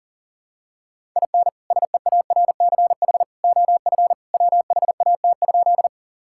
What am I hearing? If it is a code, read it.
Morse code, heard as a keyed tone, about 40 wpm, 700 Hz. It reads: IN SEARCH OF WHAT?
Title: Sentences from Top 1000 Words - Review Part 5 - 40wpm